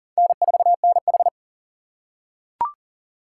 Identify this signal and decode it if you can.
Morse code, heard as a keyed tone, about 40 wpm, 700 Hz. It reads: N4NH
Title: USA Call Signs - 4 Characters - 6x Character Spacing - 40wpm